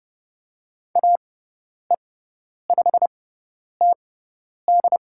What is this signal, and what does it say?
Morse code, heard as a keyed tone, about 30 wpm, spaced as for 10 wpm, 700 Hz. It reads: AE5TD
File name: Lesson 17g - Letter u - Call Signs 30wpm